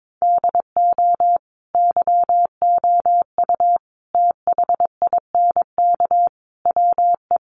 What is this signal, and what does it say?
Morse code, heard as a keyed tone, about 22 wpm, 700 Hz. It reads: DO YOU THINK WE
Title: Sentences from Top 1200 Words 22wpm